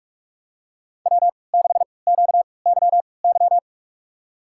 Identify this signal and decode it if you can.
Morse code, heard as a keyed tone, about 46 wpm, spaced as for 26 wpm, 700 Hz. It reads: W6QYY
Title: USA Call Signs - 5 Characters - 3x Character Spacing - 45wpm